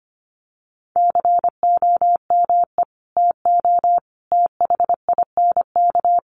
Morse code, heard as a keyed tone, about 25 wpm, 700 Hz. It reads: COME TO THINK
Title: Sets of 5 Words Encore 25wpm